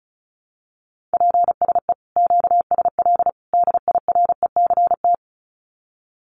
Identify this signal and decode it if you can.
Morse code, heard as a keyed tone, about 34 wpm, 700 Hz. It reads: PSE QSL DIRECT
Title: Common QSO Phrases 35wpm